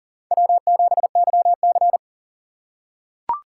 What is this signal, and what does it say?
Morse code, heard as a keyed tone, about 40 wpm, 700 Hz. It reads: W7YC
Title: USA Call Signs - 4 Characters - 2x Character Spacing - 40wpm